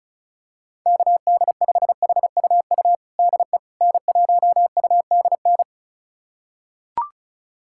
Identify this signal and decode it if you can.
Morse code, heard as a keyed tone, about 36 wpm, 700 Hz. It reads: KD5HUU DE N1UDN
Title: USA Callsign DE Callsign 35wpm